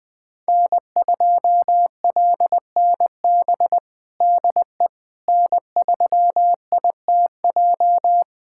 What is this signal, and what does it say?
Morse code, heard as a keyed tone, about 20 wpm, 700 Hz. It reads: N2LNB DE N3ITJ
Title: USA Callsign DE Callsign - 5 Characters - 20wpm